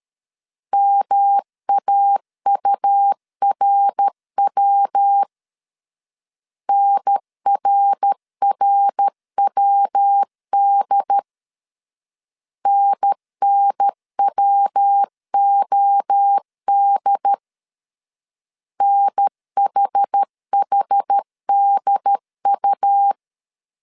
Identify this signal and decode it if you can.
Morse code, heard as a keyed tone, about 13 wpm, 800 Hz. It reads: MAURW NRRWD NNWOD NHHDU
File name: CW / DET / 1 3 1500C S C 12wpm 2750